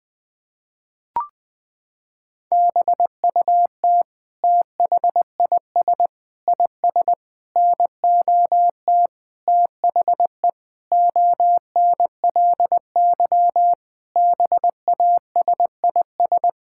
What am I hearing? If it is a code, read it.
Morse code, heard as a keyed tone, about 21 wpm, 700 Hz. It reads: BUT THIS IS NOT THE ONLY BASIS
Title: Sentences from Top 1600 Words 20wpm